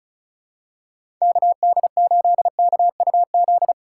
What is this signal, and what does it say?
Morse code, heard as a keyed tone, about 34 wpm, 700 Hz. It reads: KD8KUZ
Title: USA Call Signs - 6 Characters - 4x Character Spacing - 35wpm